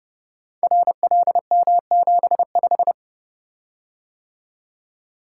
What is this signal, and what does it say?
Morse code, heard as a keyed tone, about 30 wpm, 700 Hz. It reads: RLM75
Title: Code Groups - 5 Letter:Numbers - 30wpm